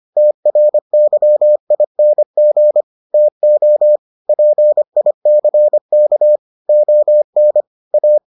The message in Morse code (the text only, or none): TRYING TO PICK ON A